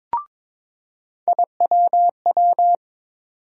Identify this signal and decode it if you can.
Morse code, heard as a keyed tone, about 22 wpm, 700 Hz. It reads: IWW
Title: USA Call Sign Postfixes - 3 Letters 22wpm